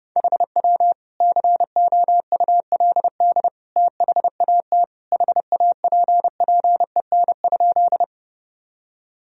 Morse code, heard as a keyed tone, about 30 wpm, 700 Hz. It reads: HW COULD THAT HAPPEN?
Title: Sentences from Top 400 Words and Abbreviations 30wpm